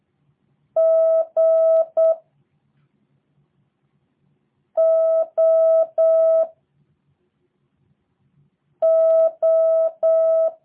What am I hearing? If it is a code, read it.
Morse code, heard as a keyed tone, about 8 wpm, 650 Hz. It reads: G O O
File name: jh8cbh / 2023cw / review1-4